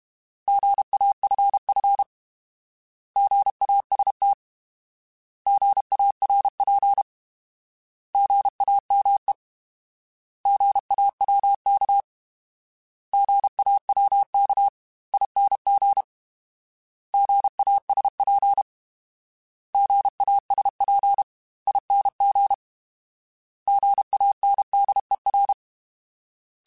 Morse code, heard as a keyed tone, about 32 wpm, 800 Hz. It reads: GAFF GAST GARP GAME GAWK GAWKING GASP GASPING GANDER
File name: Alphabetical / Alphabetical mp3s / Gs @30